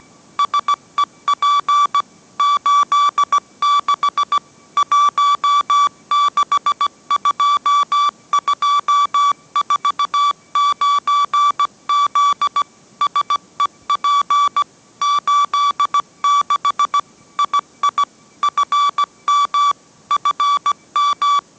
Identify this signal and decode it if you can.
Morse code, heard as a keyed tone, about 17 wpm, 1150 Hz. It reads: SEP 86 162249Z SEP 86 II FM FM